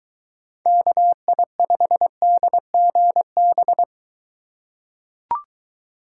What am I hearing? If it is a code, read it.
Morse code, heard as a keyed tone, about 21 wpm, 700 Hz. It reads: KI5DGB